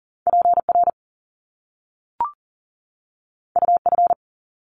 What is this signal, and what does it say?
Morse code, heard as a keyed tone, about 38 wpm, 700 Hz. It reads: PR UF
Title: Top Two Letter Combinations in Words 40wpm